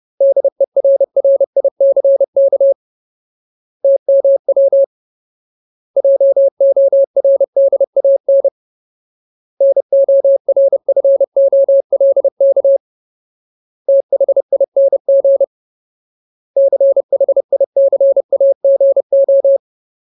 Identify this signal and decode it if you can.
Morse code, heard as a keyed tone, about 30 wpm, 550 Hz. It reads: DERRICK TMW JORDAN NORFOLK THING CHICAGO
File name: Ultimate Word List / 30WPM Ultimate